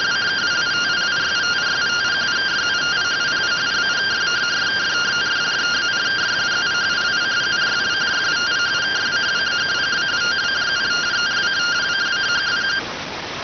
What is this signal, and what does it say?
Two-tone teleprinter signal, about 170 Hz shift.